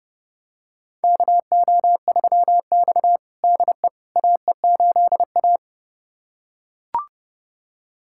Morse code, heard as a keyed tone, about 30 wpm, 700 Hz. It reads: KO3X DE AE8A
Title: USA Callsign DE Callsign - 4 Characters - 30wpm